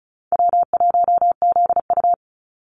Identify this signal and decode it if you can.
Morse code, heard as a keyed tone, about 34 wpm, 700 Hz. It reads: W1ZU
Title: NA SOTA Activator Callsigns 35wpm